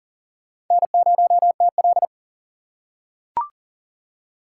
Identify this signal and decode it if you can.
Morse code, heard as a keyed tone, about 40 wpm, 700 Hz. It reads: N0TL
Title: USA Call Signs - 4 Characters - 3x Character Spacing - 40wpm